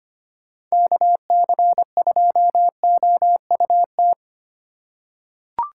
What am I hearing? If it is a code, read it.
Morse code, heard as a keyed tone, about 25 wpm, 700 Hz. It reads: KC2OUT
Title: USA Call Signs - 6 Characters - 4x Character Spacing - 25wpm